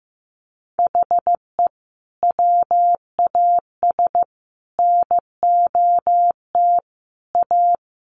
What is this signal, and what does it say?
Morse code, heard as a keyed tone, about 15 wpm, 700 Hz. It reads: HE WAS NOT A
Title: Sentences from Top 1200 Words 15wpm